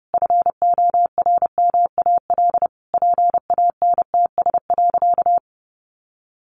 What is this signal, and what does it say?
Morse code, heard as a keyed tone, about 30 wpm, 700 Hz. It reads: FORMAL PANTS.
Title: Additional Easy Phrases 30wpm